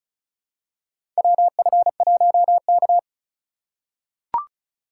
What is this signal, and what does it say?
Morse code, heard as a keyed tone, about 34 wpm, 700 Hz. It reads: WF1K
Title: USA Call Signs - 4 Characters - 2x Character Spacing - 35wpm